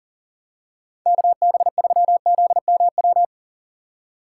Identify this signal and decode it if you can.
Morse code, heard as a keyed tone, about 40 wpm, 700 Hz. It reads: KB3ZMW
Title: USA Call Signs - 6 Characters - 3x Character Spacing - 40wpm